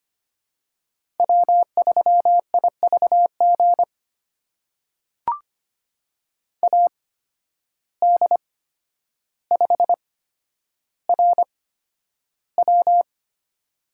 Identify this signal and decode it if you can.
Morse code, heard as a keyed tone, about 25 wpm, spaced as for 11 wpm, 700 Hz. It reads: W3IVG AD5RW
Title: USA Call Signs - 5 Characters - 8x Character Spacing - 25wpm